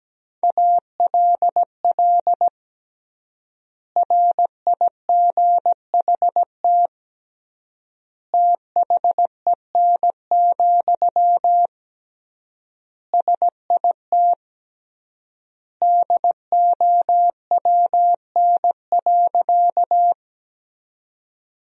Morse code, heard as a keyed tone, about 17 wpm, 700 Hz. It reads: ALL RIGHT THEN, SIT DOWN.
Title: 3x Word-Spacing - Sentences from Top 500 Words - 17wpm